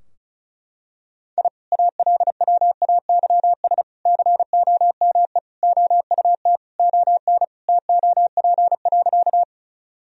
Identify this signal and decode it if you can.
Morse code, heard as a keyed tone, about 34 wpm, 700 Hz. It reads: I ALWAYS COME OUT ON TOP.
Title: Sentences from Top 1000 Words - Review Part 2 - 35wpm